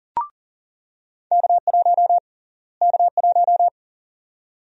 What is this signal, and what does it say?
Morse code, heard as a keyed tone, about 40 wpm, 700 Hz. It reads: K1 K1
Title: USA Call Sign Prefixes - Repeated 2x - 40wpm